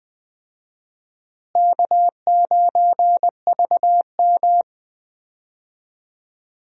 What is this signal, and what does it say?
Morse code, heard as a keyed tone, about 21 wpm, 700 Hz. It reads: K9VM
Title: USA Call Signs - 4 Characters - Repeated 3x - 20wpm